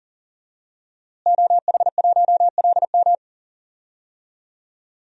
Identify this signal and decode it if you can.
Morse code, heard as a keyed tone, about 40 wpm, 700 Hz. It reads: OH1LM
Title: DXCC-100 Call Signs - 5 Characters - Repeated 2x - 40wpm